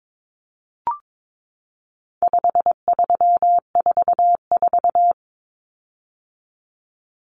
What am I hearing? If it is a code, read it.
Morse code, heard as a keyed tone, about 22 wpm, 700 Hz. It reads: 5344
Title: Common Cut Numbers - 4-Digit 22wpm